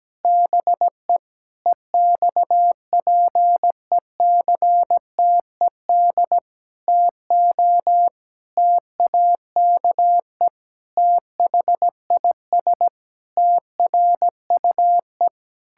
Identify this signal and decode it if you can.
Morse code, heard as a keyed tone, about 17 wpm, 700 Hz. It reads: BE EXPECTED TO TAKE THIS TRUE
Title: Pangram Challenge v2 - 17wpm